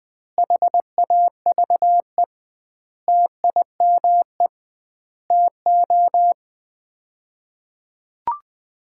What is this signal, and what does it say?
Morse code, heard as a keyed tone, about 20 wpm, 700 Hz. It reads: HAVE TIME TO
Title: 2x Word-Spacing - Sets of 5 Words - 20wpm